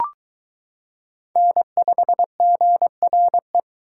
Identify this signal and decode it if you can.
Morse code, heard as a keyed tone, about 22 wpm, 700 Hz. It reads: N5GRE